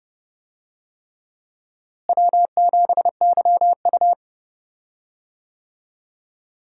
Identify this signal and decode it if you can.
Morse code, heard as a keyed tone, about 29 wpm, 700 Hz. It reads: W7YU
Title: USA Call Signs - 4 Characters - Repeated 3x - 30wpm